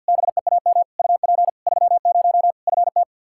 Morse code, heard as a keyed tone, about 48 wpm, 700 Hz. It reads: BEAM UP 30 FT
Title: Practice QSO - Exchange 2 - Info - 50wpm